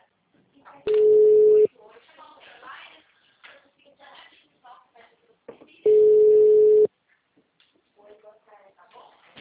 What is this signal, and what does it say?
Ringback tone (Europe, 425 Hz), with speech.